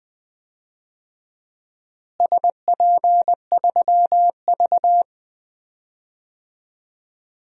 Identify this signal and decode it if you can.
Morse code, heard as a keyed tone, about 20 wpm, 700 Hz. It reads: SP3V
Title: DXCC-100 Call Signs - 4 Characters - Repeated 3x - 20wpm